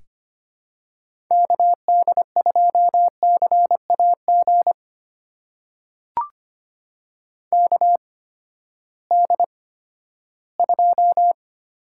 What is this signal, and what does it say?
Morse code, heard as a keyed tone, about 25 wpm, spaced as for 14 wpm, 700 Hz. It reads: KD2CAG KD2
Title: USA Call Signs - 6 Characters - 8x Character Spacing - 25wpm